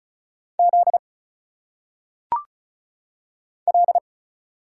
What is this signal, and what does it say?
Morse code, heard as a keyed tone, about 36 wpm, 700 Hz. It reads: Z L
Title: Single Letters - Phonetic Pronunciation 35wpm